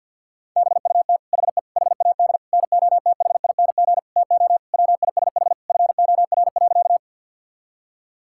Morse code, heard as a keyed tone, about 50 wpm, 700 Hz. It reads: BUT HE HAD NOTHING TO WISH FOR.
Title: Sentences from Top 1600 Words 50wpm